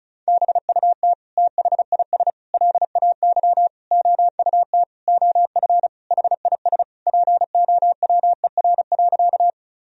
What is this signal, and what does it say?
Morse code, heard as a keyed tone, about 34 wpm, 700 Hz. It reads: BUT THIS LAY OUT OF HIS POWER.
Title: Sentences from Top 1000 Words - Review Part 2 - 35wpm